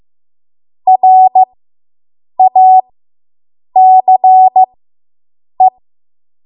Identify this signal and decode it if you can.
Morse code, heard as a keyed tone, about 15 wpm, spaced as for 7 wpm, 750 Hz. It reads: RACE